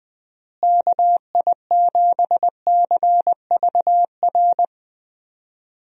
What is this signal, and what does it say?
Morse code, heard as a keyed tone, about 20 wpm, 700 Hz. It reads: KI7CVR
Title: USA Call Signs - 6 Characters - Repeated 2x - 20wpm